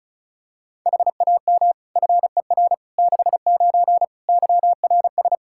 Morse code, heard as a keyed tone, about 34 wpm, 700 Hz. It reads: HAM FER 69 YRS
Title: K1USN SST QSO Practice 35wpm